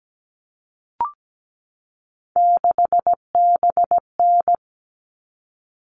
Morse code, heard as a keyed tone, about 17 wpm, 700 Hz. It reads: 6BN
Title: All Cut Numbers - 3-Digit 17wpm